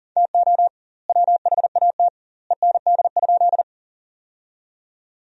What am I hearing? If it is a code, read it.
Morse code, heard as a keyed tone, about 37 wpm, 700 Hz. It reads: TO WHAT END?
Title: 2x Word-Spacing - Sentences from Top 200 Words - 40wpm